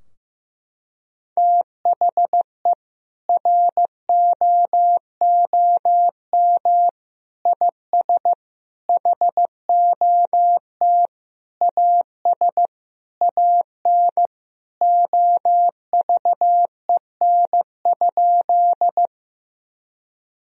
Morse code, heard as a keyed tone, about 15 wpm, 700 Hz. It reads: THE ROOM IS HOT AS AN OVEN?
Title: Sentences from Top 1000 Words - Review Part 4 - 15wpm